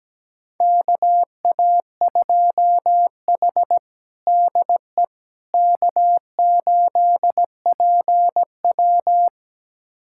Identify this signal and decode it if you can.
Morse code, heard as a keyed tone, about 17 wpm, 700 Hz. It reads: KA2H DE K8PW
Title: USA Callsign DE Callsign - 4 Characters - 17wpm